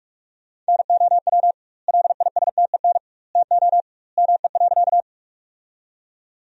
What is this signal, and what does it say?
Morse code, heard as a keyed tone, about 46 wpm, 700 Hz. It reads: NOW LISTEN TO ME.
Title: 2x Word-Spacing - Sentences from Top 400 Words - 45wpm